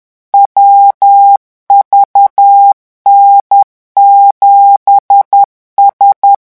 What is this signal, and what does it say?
Morse code, heard as a keyed tone, about 10 wpm, 800 Hz. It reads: WVN7S
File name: morse mp3 / revision16-vitesse 10 mots